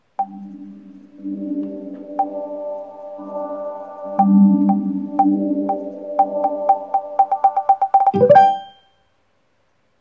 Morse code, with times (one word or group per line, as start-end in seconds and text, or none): none